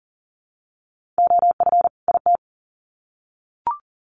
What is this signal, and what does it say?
Morse code, heard as a keyed tone, about 37 wpm, 700 Hz. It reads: OF IT
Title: Sets of 2 Words 40wpm